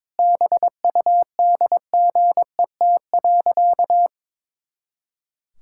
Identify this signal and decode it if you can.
Morse code, heard as a keyed tone, about 22 wpm, 700 Hz. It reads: BUDGET.
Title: Pangram Challenge v2 - 22wpm